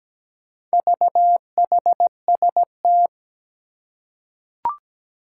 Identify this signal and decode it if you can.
Morse code, heard as a keyed tone, about 17 wpm, 700 Hz. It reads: VHST
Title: Four Challenging Letter-Number Combinations 17wpm